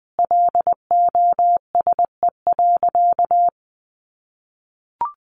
Morse code, heard as a keyed tone, about 20 wpm, 700 Hz. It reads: LOSE.
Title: Sentences from Top 1100 Words 20wpm